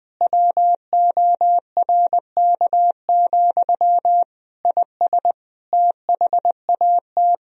Morse code, heard as a keyed tone, about 20 wpm, 700 Hz. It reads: WORK, IS THAT
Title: Sentences from Top 1000 Words - Review Part 2 - 20wpm